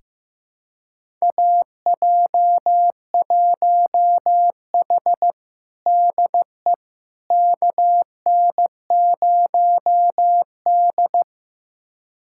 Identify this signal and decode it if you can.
Morse code, heard as a keyed tone, about 15 wpm, 700 Hz. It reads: AJ1H DE KN0D
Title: USA Callsign DE Callsign - 4 Characters - 15wpm